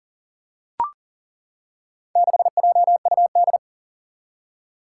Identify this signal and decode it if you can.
Morse code, heard as a keyed tone, about 40 wpm, 700 Hz. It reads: 6JUD
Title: Four Challenging Letter-Number Combinations 40wpm